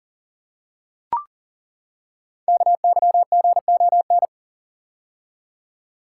Morse code, heard as a keyed tone, about 42 wpm, 700 Hz. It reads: KYGON
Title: Code Groups - 5 Letters - 40wpm